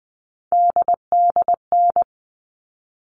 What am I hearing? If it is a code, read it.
Morse code, heard as a keyed tone, about 20 wpm, 700 Hz. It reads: DDN
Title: All Cut Numbers - 3-Digit 20wpm